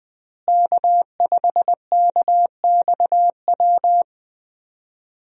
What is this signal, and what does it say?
Morse code, heard as a keyed tone, about 20 wpm, 700 Hz. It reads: K5KXW
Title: USA Call Signs - 5 Characters - Repeated 3x - 20wpm